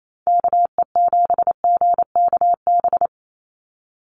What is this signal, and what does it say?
Morse code, heard as a keyed tone, about 28 wpm, 700 Hz. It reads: KE7GKB
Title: USA Call Signs 28wpm